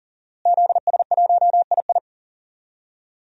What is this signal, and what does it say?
Morse code, heard as a keyed tone, about 40 wpm, 700 Hz. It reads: ZS1II
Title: DXCC-100 Call Signs - 5 Characters - Repeated 3x - 40wpm